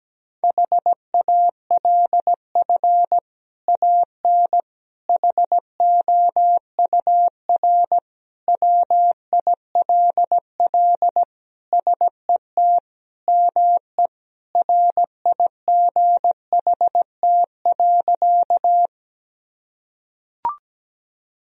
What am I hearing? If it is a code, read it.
Morse code, heard as a keyed tone, about 17 wpm, 700 Hz. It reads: HALF AN HOUR WILL SET ME RIGHT.
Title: Sentences from Top 1000 Words - Review Part 1 - 17wpm